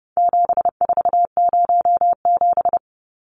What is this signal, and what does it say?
Morse code, heard as a keyed tone, about 29 wpm, 700 Hz. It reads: 7407
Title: Numbers - 4-Digit 30wpm